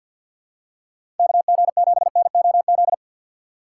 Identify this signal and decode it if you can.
Morse code, heard as a keyed tone, about 53 wpm, 700 Hz. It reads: KG7NOZ
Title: USA Call Signs - 6 Characters - 8x Character Spacing - 50wpm